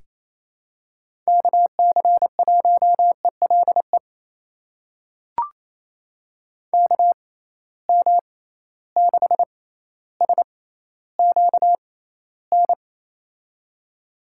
Morse code, heard as a keyed tone, about 28 wpm, spaced as for 14 wpm, 700 Hz. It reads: KC1ELE KM6SQN
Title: USA Call Signs - 6 Characters - 6x Character Spacing - 28wpm